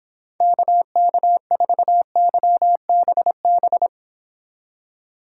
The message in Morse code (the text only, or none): KK4YBB